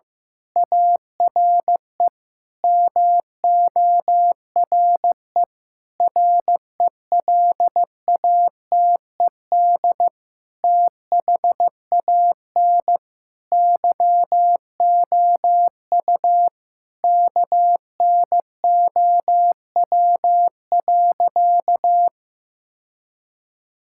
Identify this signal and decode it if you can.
Morse code, heard as a keyed tone, about 15 wpm, 700 Hz. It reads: ARE MORE RELATED THAN YOU KNOW.
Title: Sentences from Top 1000 Words - Review Part 5 - 15wpm